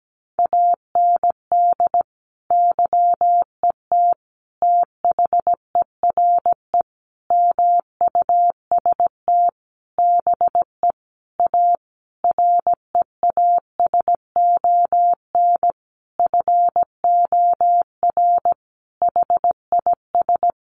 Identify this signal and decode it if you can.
Morse code, heard as a keyed tone, about 17 wpm, 700 Hz. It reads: AND YET THERE MUST BE A REASON FOR HIS
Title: Sentences from Top 1300 Words 17wpm